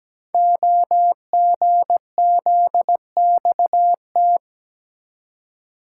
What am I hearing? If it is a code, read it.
Morse code, heard as a keyed tone, about 17 wpm, 700 Hz. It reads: OGZXT
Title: Code Groups - 5 Letters - 17wpm